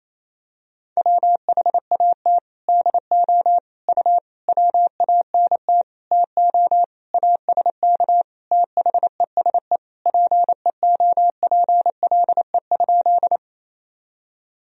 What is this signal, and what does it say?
Morse code, heard as a keyed tone, about 28 wpm, 700 Hz. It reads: WHAT DO U WANT TO ASK THESE PEOPLE?
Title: Sentences from Top 200 Words and Abbreviations 28wpm